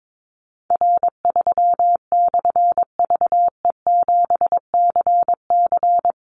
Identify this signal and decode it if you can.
Morse code, heard as a keyed tone, about 22 wpm, 700 Hz. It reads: R3/VE7CC
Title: Call Signs - 22wpm